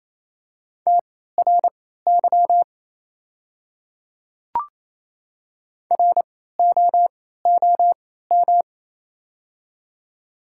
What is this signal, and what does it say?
Morse code, heard as a keyed tone, about 27 wpm, spaced as for 16 wpm, 700 Hz. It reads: TRY ROOM
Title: 3x Character Spacing - Top 500 Words - 28wpm